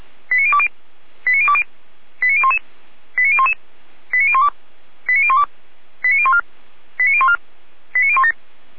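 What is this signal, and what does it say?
Five-tone selective call (ZVEI), heard as quick runs of five tones, tones of about 70 ms. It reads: 89020 89020 89011 89011 89012 89012 89014 89014 89017
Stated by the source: Five-tone ZVEI selective call received by SDR, fire brigade Orlamünde (feuerwehr-orlamuende).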